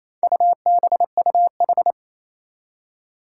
Morse code, heard as a keyed tone, about 28 wpm, 700 Hz. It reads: UBUH
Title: Four Challenging Letter-Number Combinations 28wpm